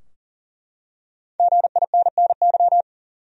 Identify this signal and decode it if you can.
Morse code, heard as a keyed tone, about 38 wpm, 700 Hz. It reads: GINNY
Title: USA Common Names - Max 5 Letters - 40wpm